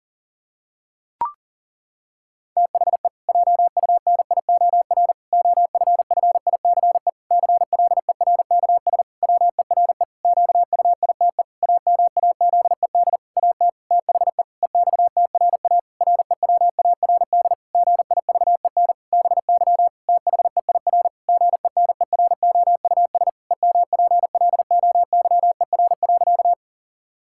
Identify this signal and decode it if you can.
Morse code, heard as a keyed tone, about 37 wpm, 700 Hz. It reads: THE JUNIOR OFFICE CLERKS WERE QUITE AMAZED AT THE EXTRA REWARD GIVEN BY THEIR GENEROUS EMPLOYER.
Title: Pangram Challenge v2 - 40wpm